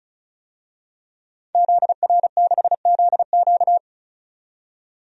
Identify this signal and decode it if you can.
Morse code, heard as a keyed tone, about 36 wpm, 700 Hz. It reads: ZR6ZQ
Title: DXCC-100 Call Signs - 5 Characters - 35wpm